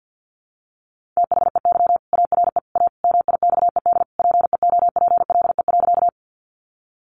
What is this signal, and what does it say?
Morse code, heard as a keyed tone, about 48 wpm, 700 Hz. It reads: THEY ARE A MIXED PEOPLE.
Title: Sentences from Top 1500 Words 50wpm